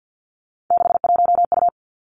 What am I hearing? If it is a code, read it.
Morse code, heard as a keyed tone, about 50 wpm, 700 Hz. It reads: 61U
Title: Three Challenging Letter-Number Combinations 50wpm